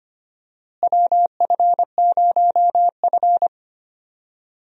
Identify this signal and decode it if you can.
Morse code, heard as a keyed tone, about 25 wpm, 700 Hz. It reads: WF0F
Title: USA Call Signs - 4 Characters - Repeated 2x - 25wpm